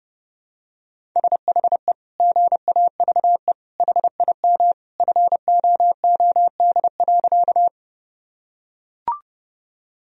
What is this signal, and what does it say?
Morse code, heard as a keyed tone, about 30 wpm, 700 Hz. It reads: SHE GAVE HIM FOOD.
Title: Sentences from Top 1000 Words - Review Part 1 - 30wpm